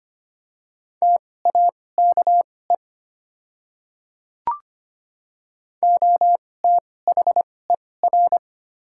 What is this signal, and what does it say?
Morse code, heard as a keyed tone, about 25 wpm, spaced as for 18 wpm, 700 Hz. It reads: TAKE OTHER
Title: 2x Character Spacing - Top 200 Words - 25wpm